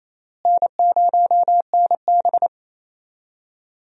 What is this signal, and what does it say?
Morse code, heard as a keyed tone, about 28 wpm, 700 Hz. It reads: N0NB
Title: USA Call Signs - 4 Characters - Repeated 3x - 28wpm